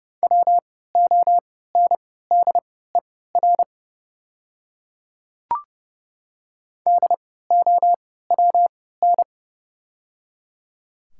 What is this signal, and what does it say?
Morse code, heard as a keyed tone, about 30 wpm, spaced as for 17 wpm, 700 Hz. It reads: WONDER DOWN
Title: 3x Character Spacing - Top 500 Words - 30wpm